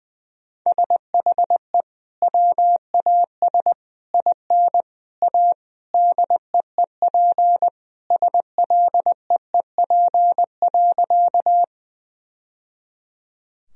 Morse code, heard as a keyed tone, about 20 wpm, 700 Hz. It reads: SHE WAS IN A DEEP SLEEP.
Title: Sentences from Top 1000 Words - Review Part 3 - 20wpm